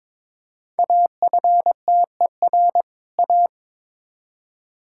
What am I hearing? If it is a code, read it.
Morse code, heard as a keyed tone, about 22 wpm, 700 Hz. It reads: AFTER A
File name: Build A Phrase - 2-Word Phrases - 22wpm